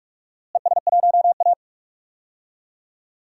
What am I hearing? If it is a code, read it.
Morse code, heard as a keyed tone, about 46 wpm, 700 Hz. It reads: ES1A
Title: DXCC-100 Call Signs - 4 Characters - Repeated 3x - 45wpm